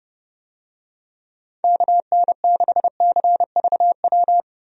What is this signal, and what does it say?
Morse code, heard as a keyed tone, about 30 wpm, 700 Hz. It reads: KN6CVW
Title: USA Call Signs - 6 Characters - 30wpm